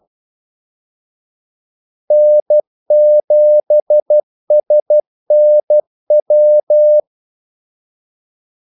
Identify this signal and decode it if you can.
Morse code, heard as a keyed tone, about 12 wpm, 600 Hz. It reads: N7SNW
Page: 2 ekstra ordmellemrum mellem kaldesignalerne